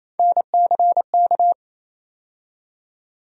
Morse code, heard as a keyed tone, about 28 wpm, 700 Hz. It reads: NCK
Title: USA Call Sign Postfixes - 3 Letters 28wpm